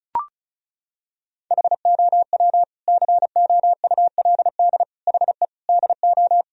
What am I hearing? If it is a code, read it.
Morse code, heard as a keyed tone, about 34 wpm, 700 Hz. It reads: HOW COULD HE DO
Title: Sentences from Top 1000 Words - Review Part 1 - 35wpm